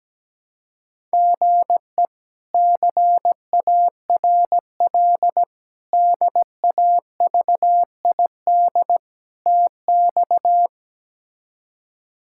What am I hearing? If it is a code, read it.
Morse code, heard as a keyed tone, about 17 wpm, 700 Hz. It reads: GE CARL DAVID TX
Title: K1USN SST QSO Practice - 17wpm